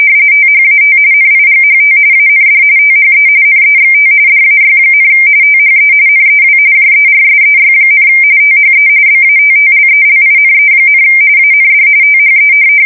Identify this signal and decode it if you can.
RTTY, about 45.45 baud. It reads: THAT NIGHT AND THE NEXT DAY, THE NEW JERSEY, THE BATTLELINE, AND THE
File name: fsk170 / USS-New-Jersey